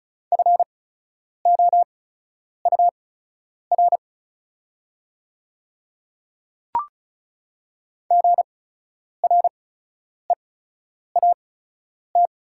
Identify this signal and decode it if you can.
Morse code, heard as a keyed tone, about 34 wpm, spaced as for 10 wpm, 700 Hz. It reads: FOUR GREAT
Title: 8x Character Spacing - Top 300 Words - 35wpm